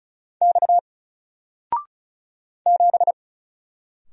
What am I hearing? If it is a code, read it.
Morse code, heard as a keyed tone, about 33 wpm, 700 Hz. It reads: X 7
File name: Single-Letter-Number-35wpm